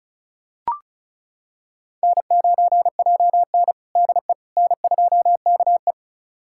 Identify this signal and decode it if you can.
Morse code, heard as a keyed tone, about 36 wpm, 700 Hz. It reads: N9JN DE N2KE
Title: USA Callsign DE Callsign - 4 Characters - 35wpm